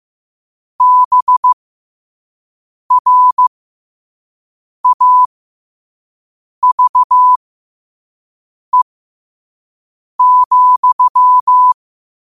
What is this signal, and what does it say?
Morse code, heard as a keyed tone, about 15 wpm, spaced as for 5 wpm, 1000 Hz. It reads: BRAVE,
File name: qotdfiles / 5wpm